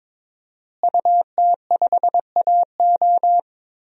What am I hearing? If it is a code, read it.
Morse code, heard as a keyed tone, about 22 wpm, 700 Hz. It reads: UT5AO
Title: DXCC-100 Call Signs - 5 Characters - Repeated 3x - 22wpm